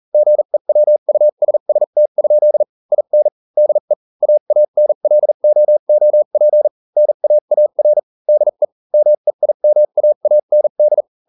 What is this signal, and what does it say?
Morse code, heard as a keyed tone, about 40 wpm, spaced as for 35 wpm, 600 Hz. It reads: GEWUSST? IN DE AANLOOP NAAR DE MEIMAAND